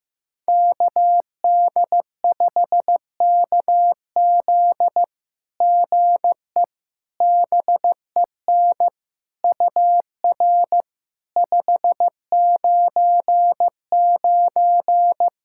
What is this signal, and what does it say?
Morse code, heard as a keyed tone, about 15 wpm, 700 Hz. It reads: KD5KZ GE BEN UR 599
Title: SOTA QSO Practice 15wpm